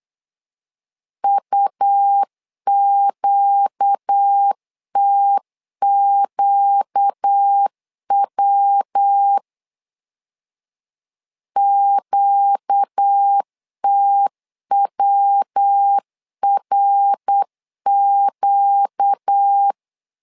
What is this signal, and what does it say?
Morse code, heard as a keyed tone, about 8 wpm, 800 Hz. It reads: UQTQW QTWRQ
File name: 4 Q 8wpm